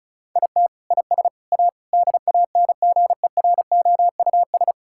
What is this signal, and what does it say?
Morse code, heard as a keyed tone, about 34 wpm, 700 Hz. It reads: IT IS A DANGEROUS
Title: Sentences from Top 1600 Words 35wpm